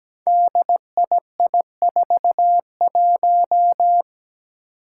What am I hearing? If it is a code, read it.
Morse code, heard as a keyed tone, about 17 wpm, 700 Hz. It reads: DII41
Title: Five Challenging Letter-Number Combinations 17wpm